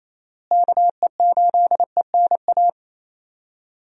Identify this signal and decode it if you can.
Morse code, heard as a keyed tone, about 28 wpm, 700 Hz. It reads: KE8ENA
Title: USA Call Signs - 6 Characters - Repeated 3x - 28wpm